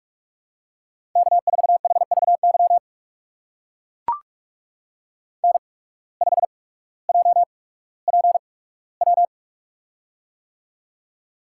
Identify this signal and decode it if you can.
Morse code, heard as a keyed tone, about 44 wpm, spaced as for 19 wpm, 700 Hz. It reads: K4HVY N5JPW
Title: USA Call Signs - 5 Characters - 8x Character Spacing - 45wpm